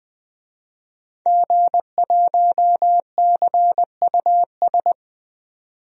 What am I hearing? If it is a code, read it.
Morse code, heard as a keyed tone, about 20 wpm, 700 Hz. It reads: G1CUS
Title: DXCC-100 Call Signs - 5 Characters - 20wpm